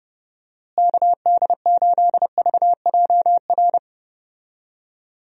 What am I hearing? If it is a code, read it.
Morse code, heard as a keyed tone, about 30 wpm, 700 Hz. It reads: KD8VJR
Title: USA Call Signs - 6 Characters - Repeated 2x - 30wpm